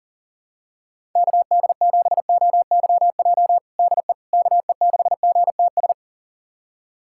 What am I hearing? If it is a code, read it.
Morse code, heard as a keyed tone, about 40 wpm, 700 Hz. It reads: KD7OYJ DE KE6GTS
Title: USA Callsign DE Callsign - 6 Characters - 40wpm